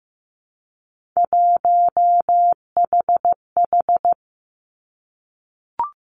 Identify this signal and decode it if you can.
Morse code, heard as a keyed tone, about 15 wpm, 700 Hz. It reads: 1HH
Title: Three Challenging Letter-Number Combinations 15wpm